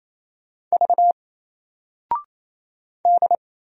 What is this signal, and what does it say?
Morse code, heard as a keyed tone, about 28 wpm, 700 Hz. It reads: V D
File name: Single-Letter-Number-28wpm